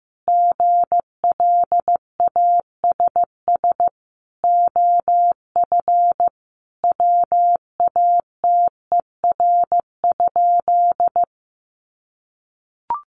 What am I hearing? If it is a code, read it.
Morse code, heard as a keyed tone, about 15 wpm, 700 Hz. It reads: GLASS OF WATER?
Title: Sentences from Top 1300 Words 15wpm